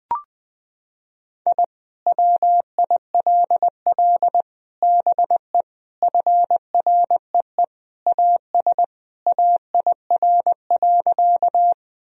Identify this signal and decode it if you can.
Morse code, heard as a keyed tone, about 20 wpm, 700 Hz. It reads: I WILL BE FREE AS AIR.
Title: Sentences from Top 1000 Words - Review Part 1 - 20wpm